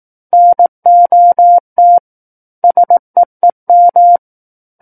Morse code, heard as a keyed tone, about 18 wpm, 700 Hz. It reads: NOT SEEM
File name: 18 WPM